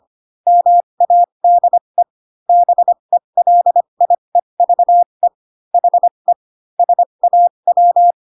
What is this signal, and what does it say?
Morse code, heard as a keyed tone, about 25 wpm, spaced as for 22 wpm, 700 Hz. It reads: MADE BELIEVE HE SAW
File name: Pinocchio / Pinocchio ch3 18 25 ik1hge